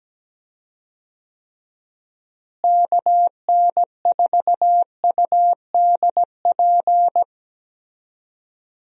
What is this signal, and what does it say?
Morse code, heard as a keyed tone, about 17 wpm, 700 Hz. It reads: KN4UDP
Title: USA Call Signs - 6 Characters - Repeated 3x - 17wpm